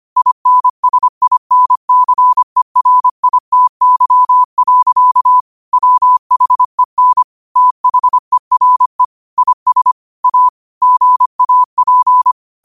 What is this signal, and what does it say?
Morse code, heard as a keyed tone, about 25 wpm, 1000 Hz. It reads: INSINCERITY. WHEN THERE IS A GAP